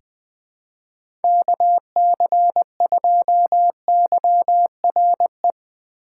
Morse code, heard as a keyed tone, about 20 wpm, 700 Hz. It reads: KC2YRE
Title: USA Call Signs - 6 Characters - 4x Character Spacing - 20wpm